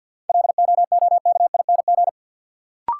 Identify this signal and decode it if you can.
Morse code, heard as a keyed tone, about 53 wpm, 700 Hz. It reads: LOOKING
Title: Sentences from Top 1000 Words - Review Part 6 - New Word Drill - 50wpm